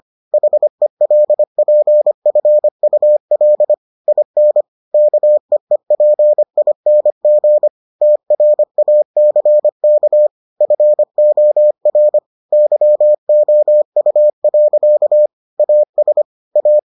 Morse code, heard as a keyed tone, about 25 wpm, 600 Hz. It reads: HELPFUL IN KEEPING TRACK FOR YOU. AS A